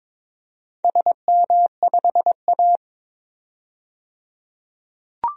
Morse code, heard as a keyed tone, about 22 wpm, 700 Hz. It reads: SM5A
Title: DXCC-100 Call Signs - 4 Characters - 22wpm